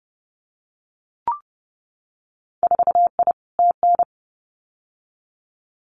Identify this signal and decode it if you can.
Morse code, heard as a keyed tone, about 30 wpm, 700 Hz. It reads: 4I TN
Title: Winter Field Day Exchanges 30wpm